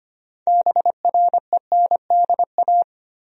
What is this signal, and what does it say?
Morse code, heard as a keyed tone, about 25 wpm, 700 Hz. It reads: BRENDA
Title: USA Common Names - 25wpm